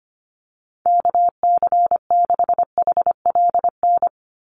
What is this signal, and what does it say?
Morse code, heard as a keyed tone, about 25 wpm, 700 Hz. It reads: KC6HLN
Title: USA Call Signs 25wpm